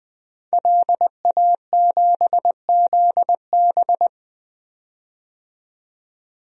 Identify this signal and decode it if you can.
Morse code, heard as a keyed tone, about 21 wpm, 700 Hz. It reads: LA7ZB
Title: DXCC-100 Call Signs - 5 Characters - 20wpm